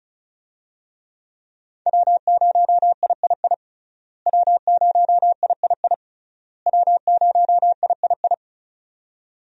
Morse code, heard as a keyed tone, about 36 wpm, 700 Hz. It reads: W0III W0III W0III
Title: USA Call Signs - 5 Characters - Repeated 3x - 35wpm